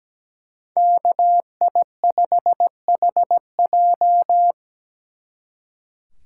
Morse code, heard as a keyed tone, about 20 wpm, 700 Hz. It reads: KI5HJ
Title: Speed Racing - USA 2x2 Call Signs 17wpm